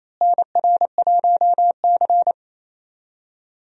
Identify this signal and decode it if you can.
Morse code, heard as a keyed tone, about 28 wpm, 700 Hz. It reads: NR1C
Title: USA Call Signs - 4 Characters - Repeated 3x - 28wpm